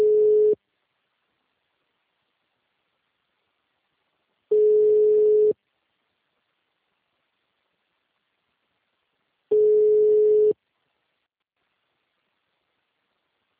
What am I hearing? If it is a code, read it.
Ringback tone (Europe, 425 Hz).